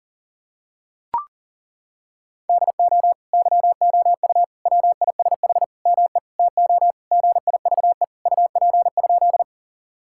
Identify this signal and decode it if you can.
Morse code, heard as a keyed tone, about 40 wpm, 700 Hz. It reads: DO YOU WISH ME TO GIVE UP?
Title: Sentences from Top 1600 Words 40wpm